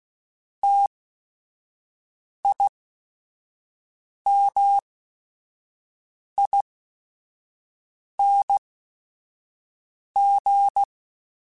Morse code, heard as a keyed tone, about 16 wpm, spaced as for 5 wpm, 800 Hz. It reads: TIMING